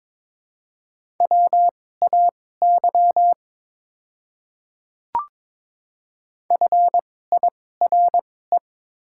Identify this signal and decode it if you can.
Morse code, heard as a keyed tone, about 22 wpm, spaced as for 16 wpm, 700 Hz. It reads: WAY FIRE
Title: 2x Character Spacing - Top 500 Words - 22wpm